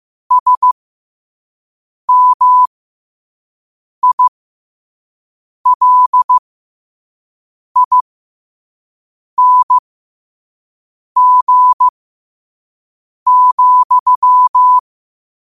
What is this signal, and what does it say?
Morse code, heard as a keyed tone, about 15 wpm, spaced as for 5 wpm, 1000 Hz. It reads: SMILING,
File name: qotdfiles / 5wpm